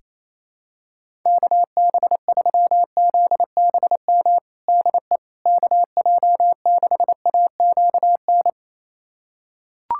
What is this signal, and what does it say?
Morse code, heard as a keyed tone, about 28 wpm, 700 Hz. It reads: KB3ZBM DE KJ6AQN
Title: USA Callsign DE Callsign - 6 Characters - 28wpm